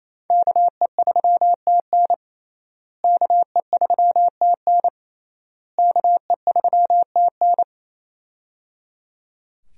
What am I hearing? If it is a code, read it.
Morse code, heard as a keyed tone, about 28 wpm, 700 Hz. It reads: KE3TN KE3TN KE3TN
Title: USA Call Signs - 5 Characters - Repeated 3x - 28wpm